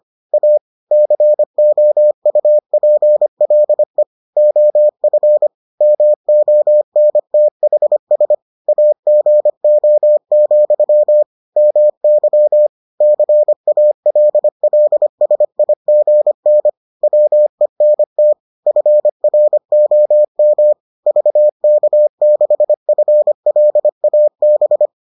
Morse code, heard as a keyed tone, about 25 wpm, 600 Hz. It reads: A COUPLE OF MONTHS AGO, MY CALLSIGN WENT FROM VK6FLAB